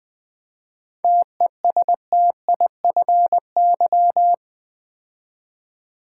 Morse code, heard as a keyed tone, about 20 wpm, 700 Hz. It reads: TESTIFY
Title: 7 Letter Words - First 3 Letters Given - 20wpm